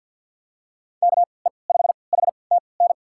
Morse code, heard as a keyed tone, about 52 wpm, spaced as for 29 wpm, 700 Hz. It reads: KE5HTN
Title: USA Call Signs - 6 Characters - 3x Character Spacing - 50wpm